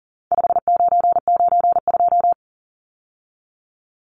Morse code, heard as a keyed tone, about 40 wpm, 700 Hz. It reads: 5992
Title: Numbers - 4-Digit 40wpm